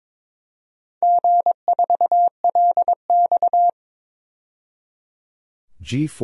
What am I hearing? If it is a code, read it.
Morse code, heard as a keyed tone, about 21 wpm, 700 Hz. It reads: G4LX
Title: DXCC-100 Call Signs - 4 Characters - 22wpm